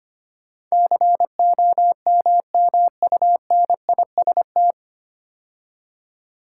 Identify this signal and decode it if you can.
Morse code, heard as a keyed tone, about 25 wpm, 700 Hz. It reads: COMMUNIST
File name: practice / 25wpm / 9 Letter Words - First 3 Letters Given - 25wpm